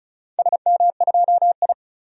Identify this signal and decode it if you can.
Morse code, heard as a keyed tone, about 36 wpm, 700 Hz. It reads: SM2I
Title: DXCC-100 Call Signs - 4 Characters - 35wpm